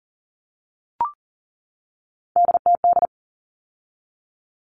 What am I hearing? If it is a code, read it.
Morse code, heard as a keyed tone, about 40 wpm, 700 Hz. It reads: DTD
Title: All Cut Numbers - 3-Digit 40wpm